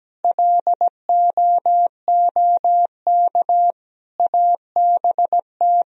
Morse code, heard as a keyed tone, about 17 wpm, 700 Hz. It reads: LOOK ABT
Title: Sentences from Top 200 Words and Abbreviations 17wpm